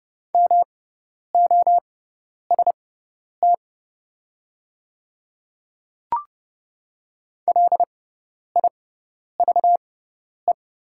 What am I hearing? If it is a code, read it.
Morse code, heard as a keyed tone, about 30 wpm, spaced as for 10 wpm, 700 Hz. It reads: MOST LIVE
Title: 6x Character Spacing - Top 200 Words - 30wpm